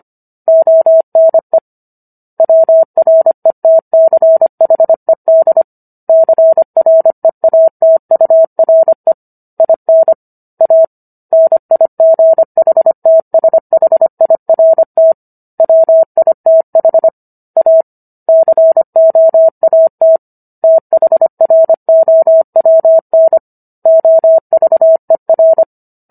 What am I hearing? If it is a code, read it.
Morse code, heard as a keyed tone, about 25 wpm, 650 Hz. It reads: ONE WRETCHED CREATURE IN A NIGHTSHIRT WITH A COAT THROWN OVER
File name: Book One / 16-Chapter 16 - 25 WPM